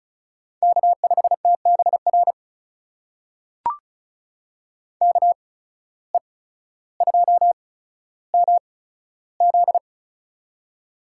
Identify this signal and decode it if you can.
Morse code, heard as a keyed tone, about 36 wpm, spaced as for 15 wpm, 700 Hz. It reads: K5TBR KE2MZ
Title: USA Call Signs - 5 Characters - 8x Character Spacing - 35wpm